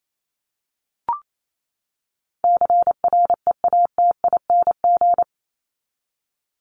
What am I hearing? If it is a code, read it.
Morse code, heard as a keyed tone, about 28 wpm, 700 Hz. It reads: CREATING
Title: ING Suffix - Top 100 Words with Base Words 28wpm